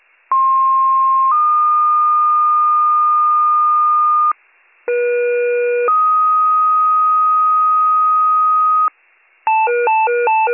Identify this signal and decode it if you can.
Two-tone page, heard as a one-second tone, then a three-second tone, about 1060 then 1220 Hz, about 480 then 1250 Hz. It is heard from 0.3 to 8.9 s.